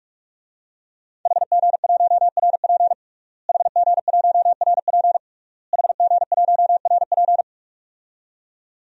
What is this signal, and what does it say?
Morse code, heard as a keyed tone, about 45 wpm, 700 Hz. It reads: HG1RP HG1RP HG1RP
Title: DXCC-100 Call Signs - 5 Characters - Repeated 3x - 45wpm